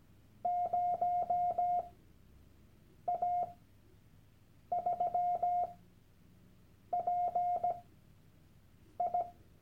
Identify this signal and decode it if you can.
Morse code, heard as a keyed tone, about 17 wpm, spaced as for 6 wpm, 710 Hz. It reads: 0A3PI
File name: Workshop Dateien / CW-Online-Kurs / Lektion 001